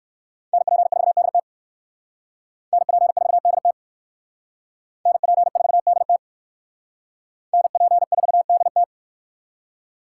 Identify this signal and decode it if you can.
Morse code, heard as a keyed tone, about 56 wpm, 700 Hz. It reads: NP4DT NP4DT NP4DT NP4DT